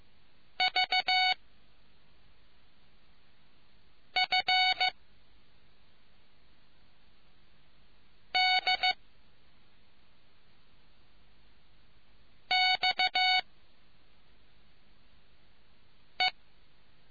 Morse code, heard as a keyed tone, about 15 wpm, 740 Hz. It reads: V F D X E